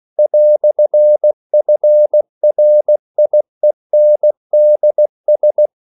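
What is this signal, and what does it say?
Morse code, heard as a keyed tone, about 16 wpm, 600 Hz. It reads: "FRIENDS
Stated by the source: English quotations sent in CW as practice at 16 words per minute.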